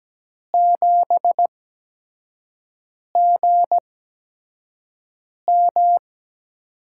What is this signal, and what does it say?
Morse code, heard as a keyed tone, about 17 wpm, spaced as for 5 wpm, 700 Hz. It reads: 7GM
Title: USA Call Signs - 5 Characters - 8x Character Spacing - 17wpm